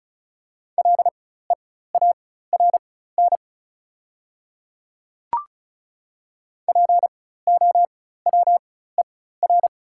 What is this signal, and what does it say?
Morse code, heard as a keyed tone, about 35 wpm, spaced as for 16 wpm, 690 Hz. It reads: LEARN POWER
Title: 4x Character Spacing - Top 500 Words - 35wpm